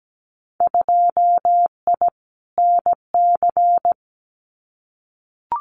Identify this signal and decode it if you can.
Morse code, heard as a keyed tone, about 17 wpm, 700 Hz. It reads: 2I NC
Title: Winter Field Day Exchanges 17wpm